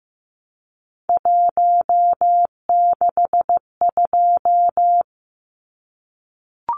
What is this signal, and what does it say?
Morse code, heard as a keyed tone, about 15 wpm, 700 Hz. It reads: 162
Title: Numbers - 3-Digit 15wpm